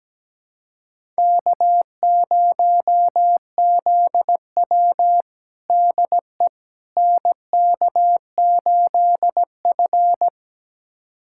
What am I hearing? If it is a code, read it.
Morse code, heard as a keyed tone, about 17 wpm, 700 Hz. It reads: K0ZW DE NK8F
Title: USA Callsign DE Callsign - 4 Characters - 17wpm